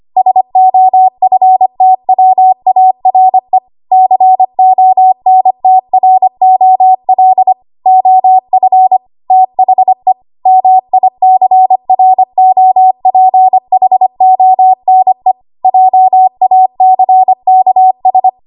Morse code, heard as a keyed tone, about 25 wpm, 750 Hz. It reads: SOFTWARE CONTROL OF THE MICROPHONE JACKS
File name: Archive / 25 WPM / 25WPM